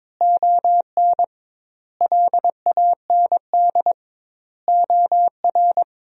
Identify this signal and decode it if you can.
Morse code, heard as a keyed tone, about 22 wpm, 700 Hz. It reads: ON LAND OR
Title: 2x Word-Spacing - Sentences from Top 500 Words - 22wpm